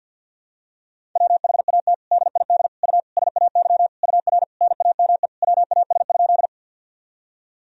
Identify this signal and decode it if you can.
Morse code, heard as a keyed tone, about 50 wpm, 700 Hz. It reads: WHAT DID U SAY UR NAME WAS?
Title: Sentences from Top 200 Words and Abbreviations 50wpm